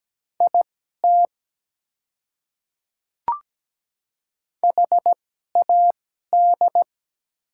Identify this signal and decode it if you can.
Morse code, heard as a keyed tone, about 17 wpm, spaced as for 12 wpm, 690 Hz. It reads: IT HAD
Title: 2x Character Spacing - Top 100 Words - 17wpm